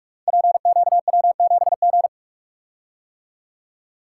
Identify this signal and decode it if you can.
Morse code, heard as a keyed tone, about 44 wpm, 700 Hz. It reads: PQW7G
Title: Code Groups - 5 Letter:Numbers - 45wpm